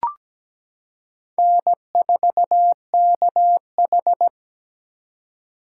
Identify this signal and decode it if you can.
Morse code, heard as a keyed tone, about 17 wpm, 700 Hz. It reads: N4KH
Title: K1USN SST QSO Practice - 17wpm